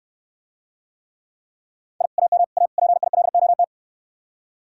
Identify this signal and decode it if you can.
Morse code, heard as a keyed tone, about 69 wpm, 700 Hz. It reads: I AM A PILOT